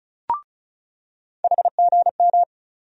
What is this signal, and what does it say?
Morse code, heard as a keyed tone, about 36 wpm, 700 Hz. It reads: HGM
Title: USA Call Sign Postfixes - 3 Letters 35wpm